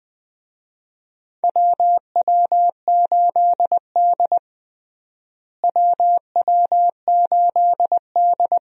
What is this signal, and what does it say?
Morse code, heard as a keyed tone, about 20 wpm, 700 Hz. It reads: WW8D WW8D
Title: USA Call Signs - 4 Characters - Repeated 2x - 20wpm